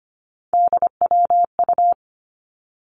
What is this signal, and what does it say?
Morse code, heard as a keyed tone, about 25 wpm, 700 Hz. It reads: DWU
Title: Three Challenging Letter-Number Combinations 25wpm